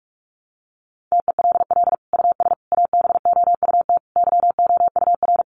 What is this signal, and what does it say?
Morse code, heard as a keyed tone, about 44 wpm, 700 Hz. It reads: TELL US ABOUT YOUR
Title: Sentences from Top 1500 Words 45wpm